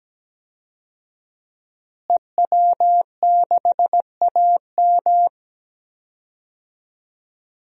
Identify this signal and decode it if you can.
Morse code, heard as a keyed tone, about 17 wpm, 700 Hz. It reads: EW6AM
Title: DXCC-100 Call Signs - 5 Characters - Repeated 3x - 17wpm